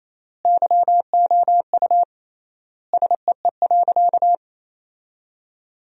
Morse code, heard as a keyed tone, about 28 wpm, 700 Hz. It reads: YOU SEE.
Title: 3x Word-Spacing - Sentences from Top 100 Words - 28wpm